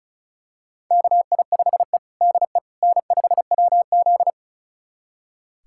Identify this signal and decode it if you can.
Morse code, heard as a keyed tone, about 36 wpm, 700 Hz. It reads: KI5E DE N5WZ
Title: USA Callsign DE Callsign - 4 Characters - 35wpm